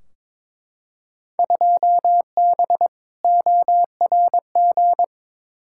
Morse code, heard as a keyed tone, about 22 wpm, 700 Hz. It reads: 2B ORG
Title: Field Day Exchanges Encore - 22wpm